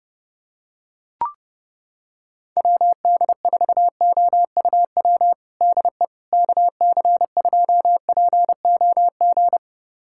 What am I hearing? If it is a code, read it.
Morse code, heard as a keyed tone, about 30 wpm, 700 Hz. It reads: WD4OUW DE KC2POG
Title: USA Callsign DE Callsign 30wpm